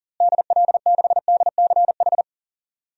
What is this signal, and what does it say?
Morse code, heard as a keyed tone, about 37 wpm, 700 Hz. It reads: DL6DCH
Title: DXCC-100 Call Signs - 6 Characters - Repeated 3x - 40wpm